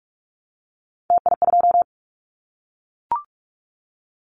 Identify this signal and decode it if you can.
Morse code, heard as a keyed tone, about 46 wpm, 700 Hz. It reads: TI2
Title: Callsign Prefixes for Top 100 DXCC - 45wpm